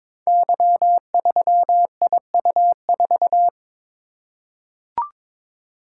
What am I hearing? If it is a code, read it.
Morse code, heard as a keyed tone, about 22 wpm, 700 Hz. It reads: Y3IU4
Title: Code Groups - 5 Letter:Numbers - 22wpm